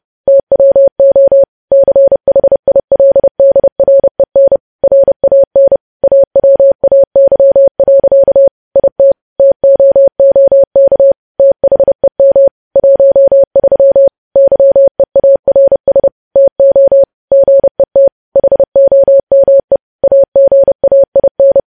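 Morse code, heard as a keyed tone, about 30 wpm, 550 Hz. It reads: TWO CHILDREN RAN AWAY. IT TOOK THEM 13 YEARS TO GET HOME AGAIN
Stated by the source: Morse news headlines sent at 30 wpm.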